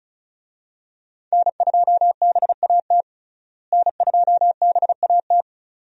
Morse code, heard as a keyed tone, about 36 wpm, 700 Hz. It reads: N2BAT N2BAT
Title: USA Call Signs - 5 Characters - Repeated 2x - 35wpm